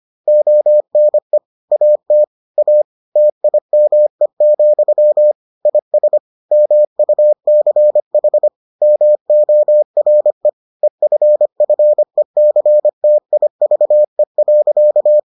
Morse code, heard as a keyed tone, about 25 wpm, 600 Hz. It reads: ONE AT A TIME, IS MUCH MORE EFFECTIVE.